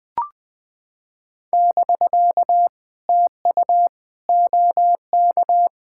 Morse code, heard as a keyed tone, about 20 wpm, 660 Hz. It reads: <BK> TU OK
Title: POTA QSO Practice - 20wpm